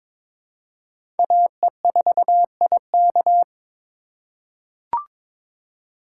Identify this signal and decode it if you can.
Morse code, heard as a keyed tone, about 22 wpm, 700 Hz. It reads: AE4IK
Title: USA Call Signs - 5 Characters - 3x Character Spacing - 22wpm